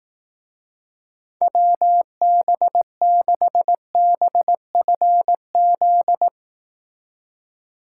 Morse code, heard as a keyed tone, about 18 wpm, 700 Hz. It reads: WB6BFZ
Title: Speed Racing - USA 2x3 Call Signs 15wpm